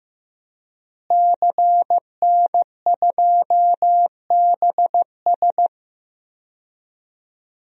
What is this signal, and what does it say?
Morse code, heard as a keyed tone, about 16 wpm, 700 Hz. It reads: CN2BS
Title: DXCC-100 Call Signs - 5 Characters - Repeated 2x - 15wpm